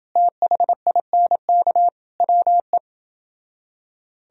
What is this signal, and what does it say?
Morse code, heard as a keyed tone, about 29 wpm, 700 Hz. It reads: THINK WE